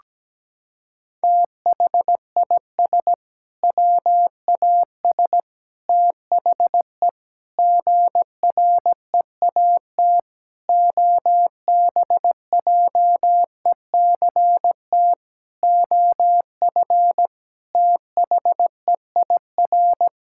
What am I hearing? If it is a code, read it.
Morse code, heard as a keyed tone, about 17 wpm, 700 Hz. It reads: THIS WAS THE GREAT OBJECT OF THEIR
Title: Sentences from Top 1000 Words - Review Part 5 - 17wpm